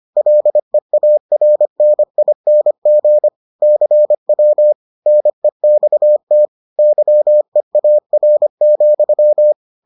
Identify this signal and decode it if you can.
Morse code, heard as a keyed tone, about 25 wpm, 600 Hz. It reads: LEARNING CW NEXT YEAR,